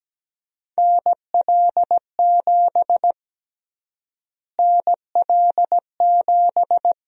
Morse code, heard as a keyed tone, about 17 wpm, 700 Hz. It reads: NL7 NL7
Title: USA Call Sign Prefixes - Repeated 2x - 17wpm